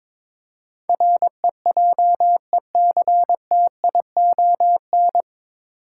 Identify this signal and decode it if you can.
Morse code, heard as a keyed tone, about 21 wpm, 700 Hz. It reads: REJECTION
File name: 9 Letter Words - First 3 Letters Given - 22wpm